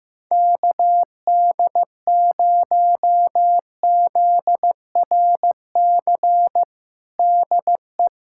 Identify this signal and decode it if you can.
Morse code, heard as a keyed tone, about 16 wpm, 700 Hz. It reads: KD0ZRC DE
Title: Practice QSO - Exchange 4 - Ending - 15wpm